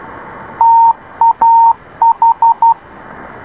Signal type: Morse code